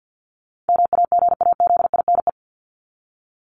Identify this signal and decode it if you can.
Morse code, heard as a keyed tone, about 50 wpm, 700 Hz. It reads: MAGAZINE
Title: Top 751-1000 Words 50wpm